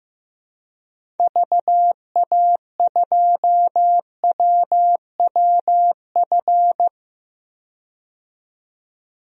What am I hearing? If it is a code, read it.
Morse code, heard as a keyed tone, about 15 wpm, 700 Hz. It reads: VA2WWF
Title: DXCC-100 Call Signs - 6 Characters - 15wpm